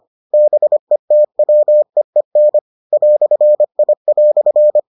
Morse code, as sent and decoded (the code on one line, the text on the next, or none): -... - .-- . -. / .-..-. .. .-..-.
BETWEEN "I"